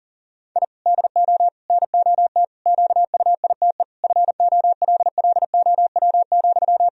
Morse code, heard as a keyed tone, about 40 wpm, 700 Hz. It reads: I DO NOT QUITE FOLLOW,
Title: Sentences from Top 1000 Words - Review Part 1 - 40wpm